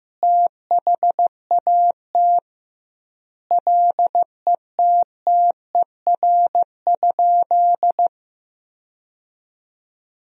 Morse code, heard as a keyed tone, about 14 wpm, 700 Hz. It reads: THAT LETTER?
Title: 2x Word-Spacing - Sentences from Top 300 Words - 15wpm